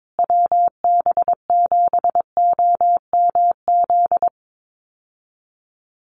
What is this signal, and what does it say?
Morse code, heard as a keyed tone, about 22 wpm, 700 Hz. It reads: WB7OMZ
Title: USA Call Signs Repeated 2x 22wpm